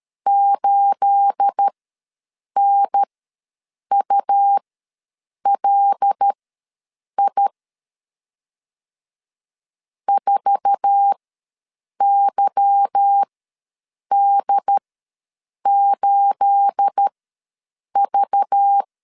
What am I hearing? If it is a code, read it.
Morse code, heard as a keyed tone, about 13 wpm, spaced as for 7 wpm, 800 Hz. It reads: 8NULI 4YD8V
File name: LEZIONI / 9 28 ws 12wpm 3354